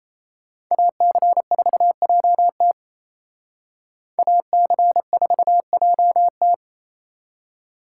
Morse code, heard as a keyed tone, about 35 wpm, 700 Hz. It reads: AC4JT AC4JT
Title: Speed Racing - USA 2x2 Call Signs 28wpm